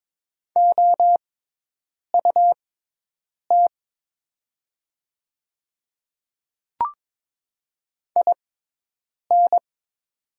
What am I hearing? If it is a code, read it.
Morse code, heard as a keyed tone, about 22 wpm, spaced as for 8 wpm, 700 Hz. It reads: OUT IN
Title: 6x Character Spacing - Top 100 Words - 22wpm